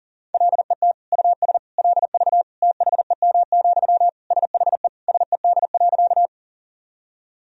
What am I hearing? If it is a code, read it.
Morse code, heard as a keyed tone, about 40 wpm, 700 Hz. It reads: LET US LV THEM, SHE SED.
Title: Sentences from Top 400 Words and Abbreviations 40wpm